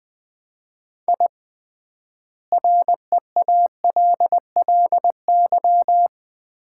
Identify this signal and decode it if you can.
Morse code, heard as a keyed tone, about 20 wpm, 700 Hz. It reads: I REALLY
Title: 3x Word-Spacing - Sets of 4 Words - 20wpm